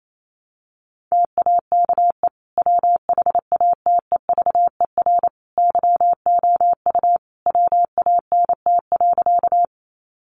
Morse code, heard as a keyed tone, about 28 wpm, 700 Hz. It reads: TAKE WHATEVER YOU WANT.
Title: Sentences from Top 1100 Words 28wpm